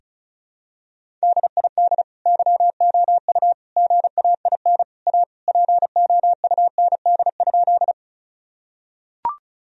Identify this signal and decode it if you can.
Morse code, heard as a keyed tone, about 36 wpm, 700 Hz. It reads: DID YOU GAIN A POUND?
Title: Sentences from Top 1000 Words - Review Part 2 - 35wpm